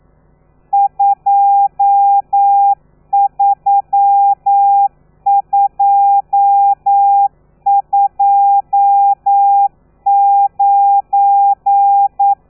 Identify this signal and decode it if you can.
Morse code, heard as a keyed tone, about 9 wpm, 800 Hz. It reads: 23229
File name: telegrafia / i2tfj / Lez12 40